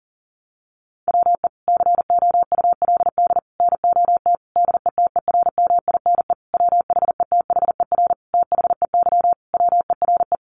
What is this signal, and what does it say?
Morse code, heard as a keyed tone, about 40 wpm, 700 Hz. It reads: WE COULD NOT DETERMINE WHETHER THEY WERE
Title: Sentences from Top 1300 Words 40wpm